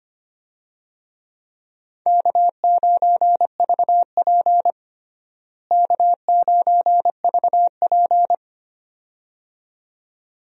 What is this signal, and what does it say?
Morse code, heard as a keyed tone, about 25 wpm, 700 Hz. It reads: K9VP K9VP
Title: USA Call Signs - 4 Characters - Repeated 2x - 25wpm